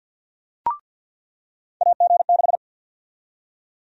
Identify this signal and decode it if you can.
Morse code, heard as a keyed tone, about 46 wpm, 700 Hz. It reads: AG6
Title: USA Call Sign Prefixes - 50wpm